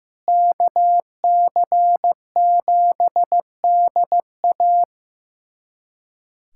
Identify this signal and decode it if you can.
Morse code, heard as a keyed tone, about 16 wpm, 700 Hz. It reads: KC7DA
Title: USA Callsign DE Callsign - 5 Characters - 15wpm